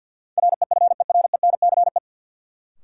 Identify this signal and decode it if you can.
Morse code, heard as a keyed tone, about 53 wpm, 700 Hz. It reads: REFERENCE
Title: Sentences from Top 1200 Words - New Word Drill Alt - 50wpm